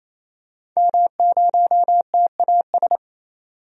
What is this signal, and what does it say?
Morse code, heard as a keyed tone, about 28 wpm, 700 Hz. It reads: M0TAS
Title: DXCC-100 Call Signs - 5 Characters - 28wpm